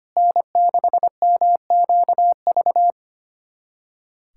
Morse code, heard as a keyed tone, about 25 wpm, 700 Hz. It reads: N6MQV
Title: USA Callsign DE Callsign 25wpm